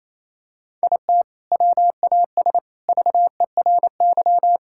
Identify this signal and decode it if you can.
Morse code, heard as a keyed tone, about 28 wpm, 700 Hz. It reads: IT WAS VERY
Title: Sentences from Top 1000 Words - Review Part 6 - 28wpm